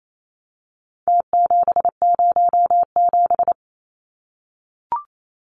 Morse code, heard as a keyed tone, about 28 wpm, 700 Hz. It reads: T707
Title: Numbers - Contest Style - 4-Digit 28wpm